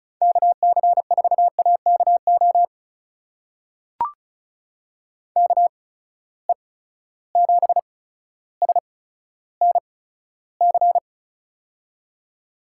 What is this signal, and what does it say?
Morse code, heard as a keyed tone, about 36 wpm, spaced as for 15 wpm, 700 Hz. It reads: KC4AKO KE7SNC
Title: USA Call Signs - 6 Characters - 8x Character Spacing - 35wpm